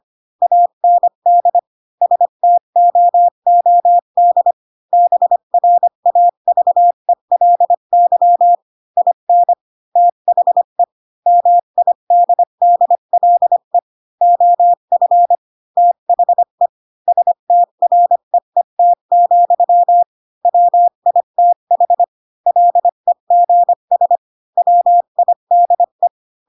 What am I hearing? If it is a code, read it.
Morse code, heard as a keyed tone, about 25 wpm, 700 Hz. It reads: AND STOOD BRAVELY IN THE MIDDLE OF THE STREET, WITH LEGS WIDE
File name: Pinocchio / Pinocchio ch3 20 25 ik1hge